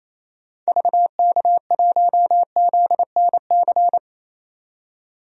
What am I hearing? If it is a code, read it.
Morse code, heard as a keyed tone, about 28 wpm, 700 Hz. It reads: VK1ZNC
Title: DXCC-100 Call Signs - 6 Characters - 28wpm